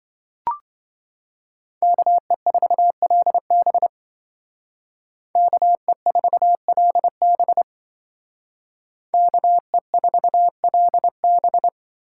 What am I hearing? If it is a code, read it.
Morse code, heard as a keyed tone, about 24 wpm, 700 Hz. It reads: KE4LB KE4LB KE4LB
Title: Speed Racing - USA Call Signs 20wpm